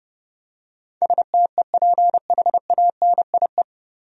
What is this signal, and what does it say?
Morse code, heard as a keyed tone, about 30 wpm, 700 Hz. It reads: STEPHANIE
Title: USA Common Names - 30wpm